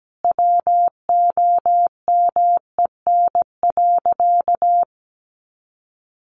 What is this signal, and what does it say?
Morse code, heard as a keyed tone, about 17 wpm, 700 Hz. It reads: WOMEN.
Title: Sentences from Top 1400 Words 17wpm